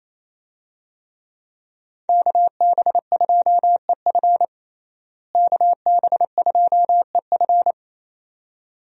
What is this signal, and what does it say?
Morse code, heard as a keyed tone, about 28 wpm, 700 Hz. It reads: KB2EF KB2EF
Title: USA Call Signs - 5 Characters - Repeated 2x - 28wpm